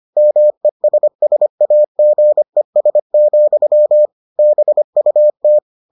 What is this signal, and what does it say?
Morse code, heard as a keyed tone, about 25 wpm, 600 Hz. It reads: MESSAGES, BUT